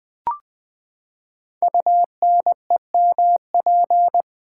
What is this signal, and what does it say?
Morse code, heard as a keyed tone, about 20 wpm, 700 Hz. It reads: UNEMP
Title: Build A Word - 5 Letters At A Time - 10-Letter Words - 20wpm